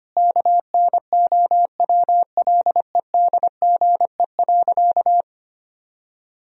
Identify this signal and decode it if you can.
Morse code, heard as a keyed tone, about 25 wpm, 700 Hz. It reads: KNOWLEDGE.
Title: Sentences from Top 1000 Words - Review Part 3 - 25wpm